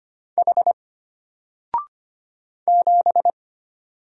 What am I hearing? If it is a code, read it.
Morse code, heard as a keyed tone, about 25 wpm, 700 Hz. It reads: H 7
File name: Single-Letter-Number-25wpm